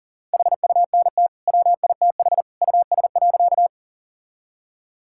Morse code, heard as a keyed tone, about 40 wpm, 700 Hz. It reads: HUNT WITH US.
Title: Sentences from Top 1600 Words 40wpm